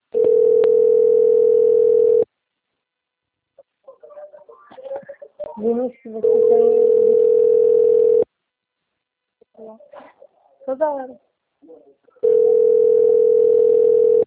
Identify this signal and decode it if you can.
Ringback tone (North America), 440 + 480 Hz, with speech.
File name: rg-1600-unknown- - - .31537